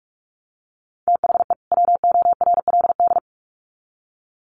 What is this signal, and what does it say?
Morse code, heard as a keyed tone, about 44 wpm, 700 Hz. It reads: THE WORLD
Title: Sets of 2 Words 45wpm